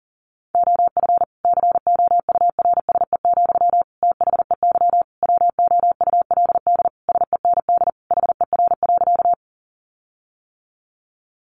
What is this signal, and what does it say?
Morse code, heard as a keyed tone, about 40 wpm, 700 Hz. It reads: OF COURSE, THEY WOULD SEND HER.
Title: Sentences from Top 1500 Words 40wpm